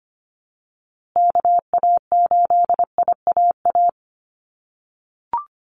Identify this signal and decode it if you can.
Morse code, heard as a keyed tone, about 25 wpm, 700 Hz. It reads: KA8IAA
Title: USA Call Signs Repeated 3x 25wpm